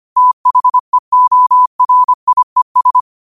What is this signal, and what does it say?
Morse code, heard as a keyed tone, about 25 wpm, 1000 Hz. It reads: THEORIES